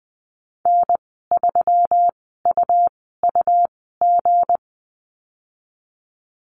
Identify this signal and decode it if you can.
Morse code, heard as a keyed tone, about 20 wpm, spaced as for 14 wpm, 700 Hz. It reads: N3UUG
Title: 173 Lesson 28g - Number 9 - Call Signs 20wpm